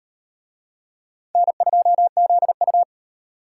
Morse code, heard as a keyed tone, about 38 wpm, 700 Hz. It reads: N2ZU